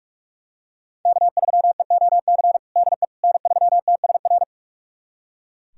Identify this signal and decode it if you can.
Morse code, heard as a keyed tone, about 42 wpm, 700 Hz. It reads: K3EOC DE N3TSR
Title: USA Callsign DE Callsign - 5 Characters - 45wpm